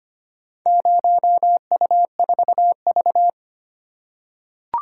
Morse code, heard as a keyed tone, about 25 wpm, 700 Hz. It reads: 0U4V
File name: practice / 25wpm / Four Letter-Number Combinations - 25wpm